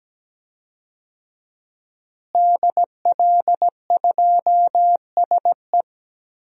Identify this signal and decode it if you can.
Morse code, heard as a keyed tone, about 17 wpm, 700 Hz. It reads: DL2SE
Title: DXCC-100 Call Signs - 5 Characters - Repeated 2x - 17wpm